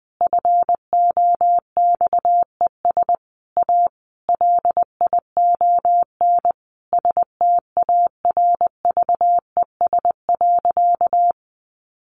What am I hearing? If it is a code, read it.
Morse code, heard as a keyed tone, about 20 wpm, 700 Hz. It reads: FOXES A LION STARVES.
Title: A Sound Of Thunder 20wpm